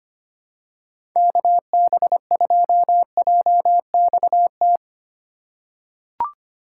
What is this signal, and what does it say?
Morse code, heard as a keyed tone, about 25 wpm, 700 Hz. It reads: KB2JXT
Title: USA Call Signs - 6 Characters - 4x Character Spacing - 25wpm